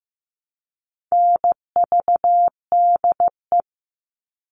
Morse code, heard as a keyed tone, about 15 wpm, 700 Hz. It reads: NVDE
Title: All Cut Numbers - 4-Digit 15wpm